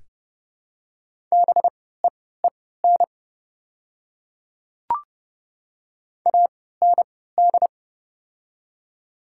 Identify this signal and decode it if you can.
Morse code, heard as a keyed tone, about 30 wpm, spaced as for 17 wpm, 700 Hz. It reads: BEEN AND
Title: 3x Character Spacing - Top 100 Words - 30wpm